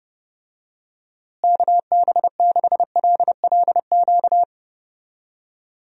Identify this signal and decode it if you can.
Morse code, heard as a keyed tone, about 30 wpm, 700 Hz. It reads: KB6LLQ
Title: USA Call Signs - 6 Characters - Repeated 3x - 30wpm